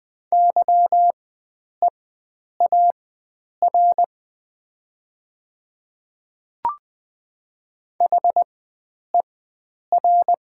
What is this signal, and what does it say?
Morse code, heard as a keyed tone, about 20 wpm, spaced as for 9 wpm, 700 Hz. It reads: YEAR HER
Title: 4x Character Spacing - Top 500 Words - 20wpm